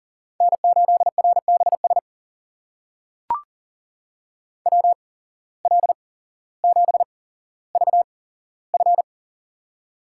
Morse code, heard as a keyed tone, about 39 wpm, spaced as for 17 wpm, 700 Hz. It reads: N8RBS WL7VF
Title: USA Call Signs - 5 Characters - 8x Character Spacing - 40wpm